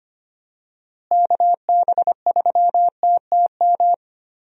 Morse code, heard as a keyed tone, about 25 wpm, 700 Hz. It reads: KB3TTM
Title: USA Call Signs - 6 Characters - Repeated 3x - 25wpm